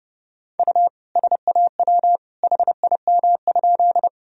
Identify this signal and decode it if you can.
Morse code, heard as a keyed tone, about 32 wpm, 700 Hz. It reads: U SAW HIM?
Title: Sentences from Top 300 Words and Abbreviations 30wpm